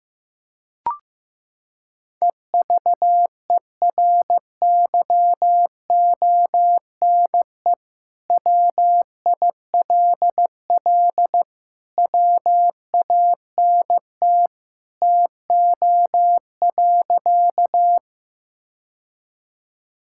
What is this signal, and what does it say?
Morse code, heard as a keyed tone, about 16 wpm, 700 Hz. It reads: EVERYONE WILL WANT TO.
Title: Sentences from Top 1000 Words - Review Part 5 - 15wpm